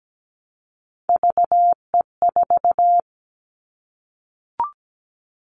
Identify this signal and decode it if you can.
Morse code, heard as a keyed tone, about 17 wpm, 700 Hz. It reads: VE4
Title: Callsign Prefixes for Top 50 DXCC - 17wpm